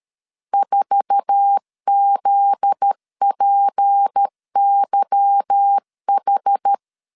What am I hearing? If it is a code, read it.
Morse code, heard as a keyed tone, about 13 wpm, 800 Hz. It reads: 4ZPYH
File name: LEZIONI / 8 YJ45 / 8 YJ4 12wpm 2151